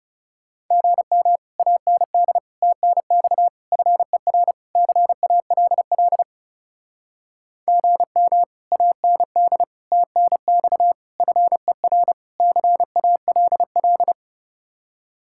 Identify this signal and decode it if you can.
Morse code, heard as a keyed tone, about 36 wpm, 700 Hz. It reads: GM AND TNX FER CALL GM AND TNX FER CALL
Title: Speed-Racing Common QSO Phrases - 30wpm